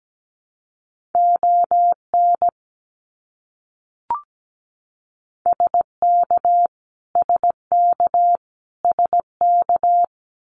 Morse code, heard as a keyed tone, about 17 wpm, 700 Hz. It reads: ON SK SK SK
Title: Canadian Provinces and Territories - Repeated 3x - 17wpm